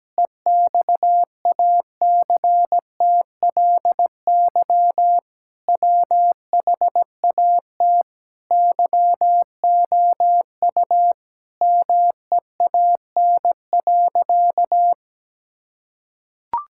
Morse code, heard as a keyed tone, about 17 wpm, 700 Hz. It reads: EXACTLY WHAT YOU MEAN.
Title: Sentences from Top 1000 Words - Review Part 6 - 17wpm